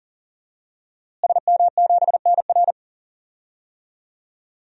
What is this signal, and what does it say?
Morse code, heard as a keyed tone, about 40 wpm, 700 Hz. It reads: SM7NR
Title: DXCC-100 Call Signs - 5 Characters - Repeated 3x - 40wpm